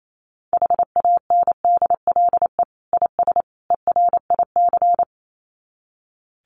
Morse code, heard as a keyed tone, about 28 wpm, 700 Hz. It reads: HANDLE IS ERIC
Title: Common QSO Phrases 28wpm